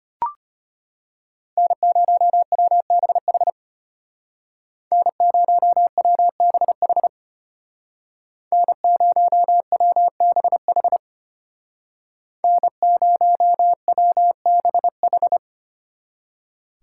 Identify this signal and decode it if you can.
Morse code, heard as a keyed tone, about 33 wpm, 700 Hz. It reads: N0WBH N0WBH N0WBH N0WBH